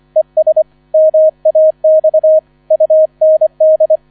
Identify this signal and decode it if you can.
Morse code, heard as a keyed tone, about 25 wpm, 630 Hz. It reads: ES MAX UND